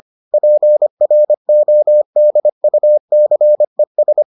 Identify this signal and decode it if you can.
Morse code, heard as a keyed tone, about 25 wpm, 600 Hz. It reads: PRODUCES